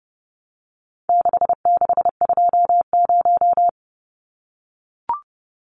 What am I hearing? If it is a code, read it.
Morse code, heard as a keyed tone, about 30 wpm, 700 Hz. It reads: 6620
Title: Numbers - 4-Digit 30wpm